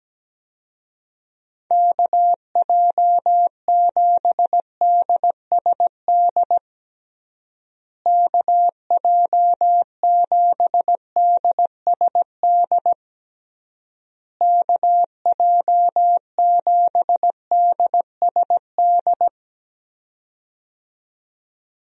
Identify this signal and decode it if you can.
Morse code, heard as a keyed tone, about 17 wpm, 700 Hz. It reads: KJ7DSD KJ7DSD KJ7DSD
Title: USA Call Signs - 6 Characters - Repeated 3x - 17wpm